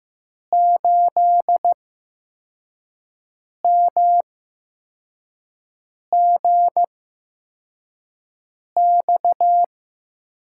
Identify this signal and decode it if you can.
Morse code, heard as a keyed tone, about 15 wpm, spaced as for 4 wpm, 700 Hz. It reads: 8MGX
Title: USA Call Signs - 6 Characters - 8x Character Spacing - 15wpm